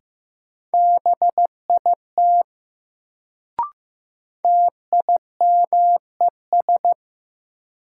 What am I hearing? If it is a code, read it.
Morse code, heard as a keyed tone, about 15 wpm, 700 Hz. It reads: BIT TIMES
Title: Sets of 3 Words - New Word Drill - 15wpm